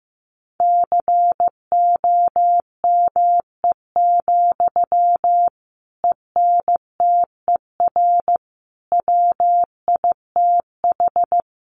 Morse code, heard as a keyed tone, about 15 wpm, 700 Hz. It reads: COME, ENTER WITH
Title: Sentences from Top 1300 Words 15wpm